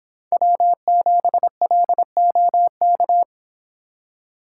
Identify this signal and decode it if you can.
Morse code, heard as a keyed tone, about 26 wpm, 700 Hz. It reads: W7LOK